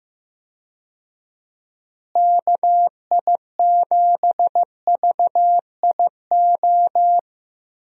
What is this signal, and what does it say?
Morse code, heard as a keyed tone, about 15 wpm, 700 Hz. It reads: KI7VIO
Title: USA Call Signs - 6 Characters - Repeated 3x - 15wpm